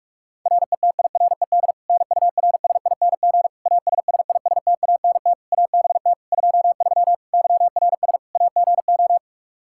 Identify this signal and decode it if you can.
Morse code, heard as a keyed tone, about 46 wpm, 700 Hz. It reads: RETIRED NURSING ASSISTANT ABT 23 YRS AGO
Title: Practice QSO - Exchange 3 - More Info - 45wpm